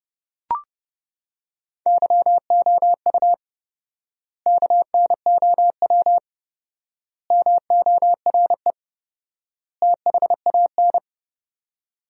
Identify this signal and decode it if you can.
Morse code, heard as a keyed tone, about 32 wpm, 700 Hz. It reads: YOU KNOW MORE THAN
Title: 4x Word-Spacing - Sentences from Top 100 Words - 30wpm